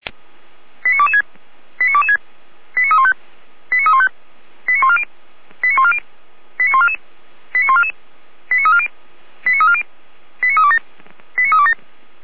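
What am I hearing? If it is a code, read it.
Five-tone selective call (ZVEI), heard as quick runs of five tones, tones of about 70 ms. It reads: 89226 89226 89315 89315 89140 89140 89144 89144 89340 89340 89327 89327
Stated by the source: Received on an SDR, ZVEI five-tone selective call, channel 455.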